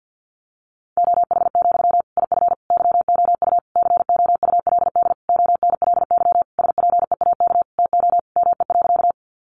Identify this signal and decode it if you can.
Morse code, heard as a keyed tone, about 50 wpm, 700 Hz. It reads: OH, IF YOU COULD ONLY SPEAK TO ME.
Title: Sentences from Top 1100 Words 50wpm